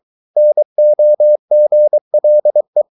Morse code, heard as a keyed tone, about 23 wpm, 600 Hz. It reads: NOGLE
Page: Ferie 23 wpm